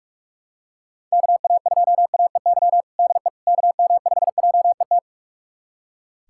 Morse code, heard as a keyed tone, about 44 wpm, 700 Hz. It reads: KA2AEY DE KM5JET
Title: USA Callsign DE Callsign - 6 Characters - 45wpm